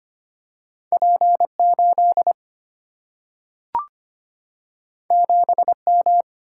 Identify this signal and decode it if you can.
Morse code, heard as a keyed tone, about 25 wpm, 690 Hz. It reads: P8 7M
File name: Two Letter-Number Combinations - 25wpm